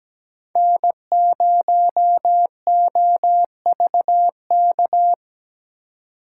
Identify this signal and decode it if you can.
Morse code, heard as a keyed tone, about 17 wpm, 700 Hz. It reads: N0OVK
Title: USA Call Signs - 4x Character Spacing - 17wpm